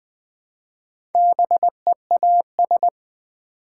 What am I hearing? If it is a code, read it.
Morse code, heard as a keyed tone, about 20 wpm, 700 Hz. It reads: BEAS
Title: Build A Word - 1 Letter At A Time - 3 to 5-Letter Words - 20wpm